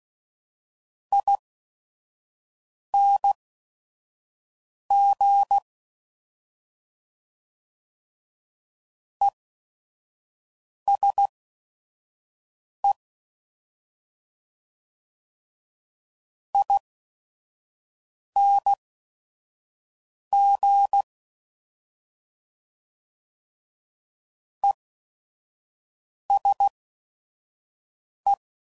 Morse code, heard as a keyed tone, about 16 wpm, spaced as for 5 wpm, 800 Hz. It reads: ING ESE ING ESE
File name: Mixed / Short Runs / ESE ING